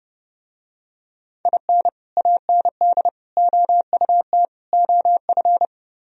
Morse code, heard as a keyed tone, about 30 wpm, 700 Hz. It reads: IN AND OUT OF